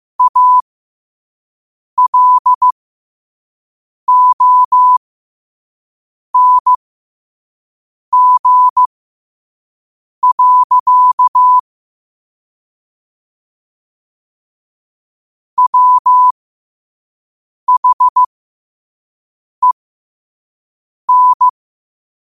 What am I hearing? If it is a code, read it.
Morse code, heard as a keyed tone, about 15 wpm, spaced as for 5 wpm, 1000 Hz. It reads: ALONG. WHEN
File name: qotdfiles / 5wpm